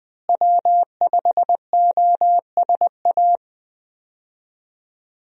Morse code, heard as a keyed tone, about 20 wpm, 700 Hz. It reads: W5OSA
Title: USA Call Signs - 5 Characters - Repeated 2x - 20wpm